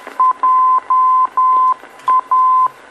Morse code, heard as a keyed tone, about 10 wpm, 1030 Hz. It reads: JA
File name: beacon program / tracks / 01 JA Beacon